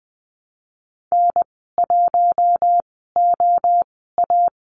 Morse code, heard as a keyed tone, about 20 wpm, spaced as for 15 wpm, 700 Hz. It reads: N1OA
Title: 069 Lesson 12g - Letter d - Call Signs 20wpm